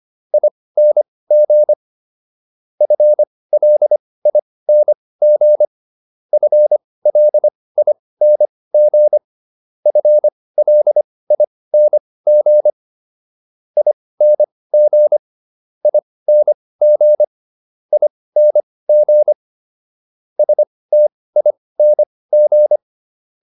Morse code, heard as a keyed tone, about 25 wpm, spaced as for 18 wpm, 600 Hz. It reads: ING FLING FLING FLING ING ING ING STING